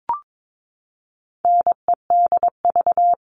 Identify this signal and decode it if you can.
Morse code, heard as a keyed tone, about 22 wpm, 700 Hz. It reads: NEDV
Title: All Cut Numbers - 4-Digit 22wpm